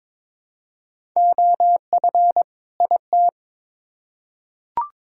Morse code, heard as a keyed tone, about 22 wpm, 700 Hz. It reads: OF IT
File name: Build A Phrase - 2-Word Phrases - 22wpm